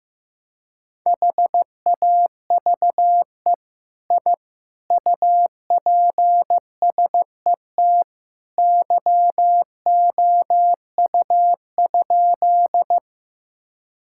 Morse code, heard as a keyed tone, about 14 wpm, 700 Hz. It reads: HAVE I UPSET YOU?
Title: Sentences from Top 1000 Words - Review Part 5 - 15wpm